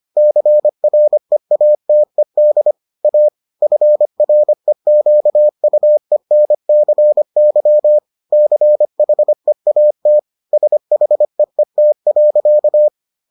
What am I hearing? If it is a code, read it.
Morse code, heard as a keyed tone, about 25 wpm, 600 Hz. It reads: CREATED A FREQUENCY CHEAT SHEET.